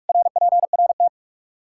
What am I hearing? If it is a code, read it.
Morse code, heard as a keyed tone, about 46 wpm, 700 Hz. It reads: RPRT
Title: Practice QSO - Exchange 2 - Info - 45wpm